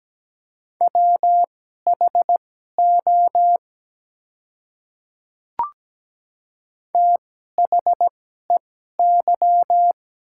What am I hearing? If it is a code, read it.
Morse code, heard as a keyed tone, about 17 wpm, spaced as for 12 wpm, 700 Hz. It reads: WHO THEY
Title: 2x Character Spacing - Top 100 Words - 17wpm